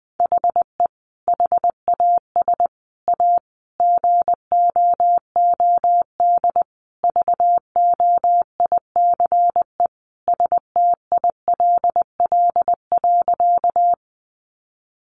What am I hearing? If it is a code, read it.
Morse code, heard as a keyed tone, about 20 wpm, 700 Hz. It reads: HE HAS A GOOD VOICE STILL.
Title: Sentences from Top 500 Words 20wpm